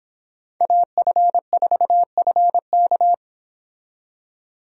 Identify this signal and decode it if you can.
Morse code, heard as a keyed tone, about 26 wpm, 700 Hz. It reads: AF4FK